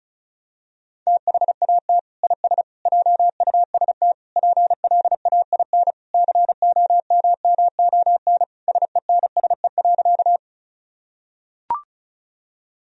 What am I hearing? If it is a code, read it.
Morse code, heard as a keyed tone, about 36 wpm, 700 Hz. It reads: THAT IS JUST PLAIN COMMON SENSE.
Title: Sentences from Top 1000 Words - Review Part 4 - 35wpm